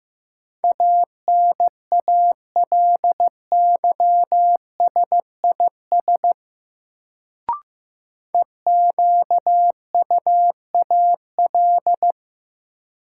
Morse code, heard as a keyed tone, about 15 wpm, 700 Hz. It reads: ANALYSIS EQUAL
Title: Sentences from Top 1400 Words - New Word Drill Alt - 15wpm